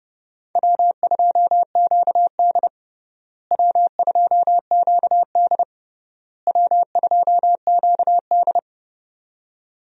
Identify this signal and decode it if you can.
Morse code, heard as a keyed tone, about 30 wpm, 700 Hz. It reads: W2QD W2QD W2QD
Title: USA Call Signs - 4 Characters - Repeated 3x - 30wpm